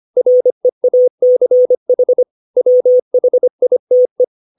Morse code, heard as a keyed tone, about 25 wpm, 500 Hz. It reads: REACH WHITE
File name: 632 words / 632 Words 25WPM